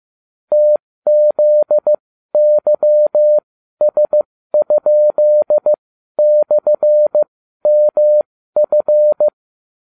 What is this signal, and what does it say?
Morse code, heard as a keyed tone, about 15 wpm, spaced as for 12 wpm, 600 Hz. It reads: TZYS?/MF